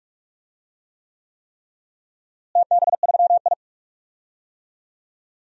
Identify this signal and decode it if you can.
Morse code, heard as a keyed tone, about 46 wpm, 700 Hz. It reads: TB3I
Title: DXCC-100 Call Signs - 4 Characters - Repeated 3x - 45wpm